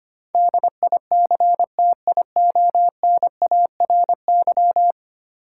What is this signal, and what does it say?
Morse code, heard as a keyed tone, about 25 wpm, 700 Hz. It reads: DICTIONARY
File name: Ten Letter Words - 25wpm